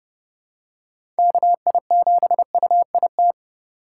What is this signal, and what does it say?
Morse code, heard as a keyed tone, about 30 wpm, 700 Hz. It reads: KI7UIT
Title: USA Call Signs - 6 Characters - 4x Character Spacing - 30wpm